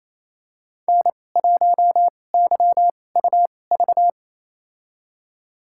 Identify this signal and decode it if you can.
Morse code, heard as a keyed tone, about 28 wpm, spaced as for 20 wpm, 700 Hz. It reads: N1YUV
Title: USA Call Signs - 5 Characters - 2x Character Spacing - 28wpm